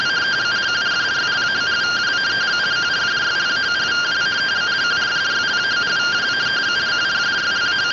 Two-tone teleprinter signal, about 170 Hz shift.